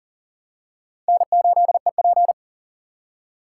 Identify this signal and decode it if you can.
Morse code, heard as a keyed tone, about 40 wpm, 700 Hz. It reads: N8EP
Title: USA Call Signs - 6x Character Spacing - 40wpm